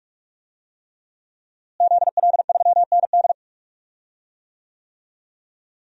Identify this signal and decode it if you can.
Morse code, heard as a keyed tone, about 46 wpm, 700 Hz. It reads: ZL3ND
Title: DXCC-100 Call Signs - 5 Characters - Repeated 3x - 45wpm